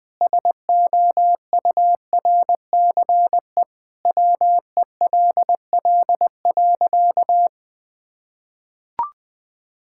Morse code, heard as a keyed tone, about 20 wpm, 700 Hz. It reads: SOURCE WELL.
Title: Sentences from Top 1000 Words - Review Part 5 - 20wpm